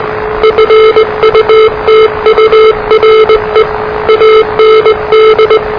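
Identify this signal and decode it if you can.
Morse code, heard as a keyed tone, about 20 wpm, 440 Hz, with noise. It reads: FUTURE AND
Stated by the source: Morse challenge from NZ Net, the national amateur radio net of Aotearoa/New Zealand, which meets on 3535.0 kHz.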